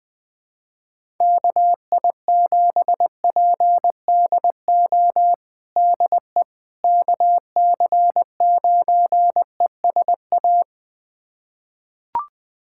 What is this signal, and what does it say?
Morse code, heard as a keyed tone, about 21 wpm, 700 Hz. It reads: KI7PDO DE KC9ESA
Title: USA Callsign DE Callsign - 6 Characters - 20wpm